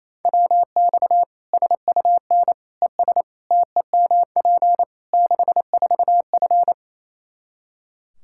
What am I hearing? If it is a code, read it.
Morse code, heard as a keyed tone, about 28 wpm, 700 Hz. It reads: WX SUN ES TEMP 64F
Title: Practice QSO - Exchange 2 - Info - 28wpm